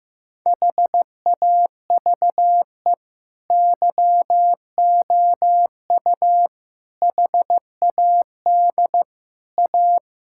Morse code, heard as a keyed tone, about 15 wpm, 700 Hz. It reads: HAVE YOU HAD A
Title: Sentences from Top 1000 Words - Review Part 6 - 15wpm